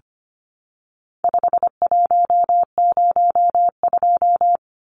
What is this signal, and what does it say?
Morse code, heard as a keyed tone, about 25 wpm, 700 Hz. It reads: 5102
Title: Numbers - 4-Digit 25wpm